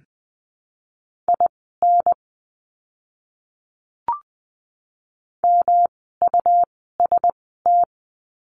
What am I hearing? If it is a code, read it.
Morse code, heard as a keyed tone, about 20 wpm, spaced as for 14 wpm, 700 Hz. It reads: IN MUST
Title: 237 Lesson 38f - Letter q - Words 20wpm